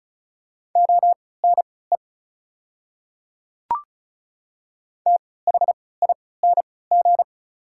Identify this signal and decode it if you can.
Morse code, heard as a keyed tone, about 36 wpm, spaced as for 20 wpm, 690 Hz. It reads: ONE THING
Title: 3x Character Spacing - Top 100 Words - 35wpm